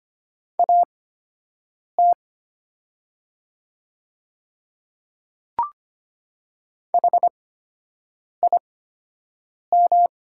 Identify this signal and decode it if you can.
Morse code, heard as a keyed tone, about 25 wpm, spaced as for 7 wpm, 700 Hz. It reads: AT HIM
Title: 8x Character Spacing - Top 100 Words - 25wpm